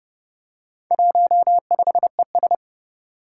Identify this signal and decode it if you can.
Morse code, heard as a keyed tone, about 30 wpm, 700 Hz. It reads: 15ES
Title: Four Challenging Letter-Number Combinations 30wpm